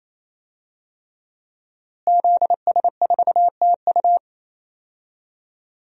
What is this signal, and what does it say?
Morse code, heard as a keyed tone, about 27 wpm, 700 Hz. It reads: ZS4TU
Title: DXCC-100 Call Signs - 5 Characters - Repeated 3x - 28wpm